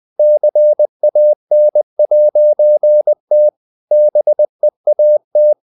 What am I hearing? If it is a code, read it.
Morse code, heard as a keyed tone, about 20 wpm, 600 Hz. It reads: CAN'T BEAT
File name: CW / Practice / EnglishSayings20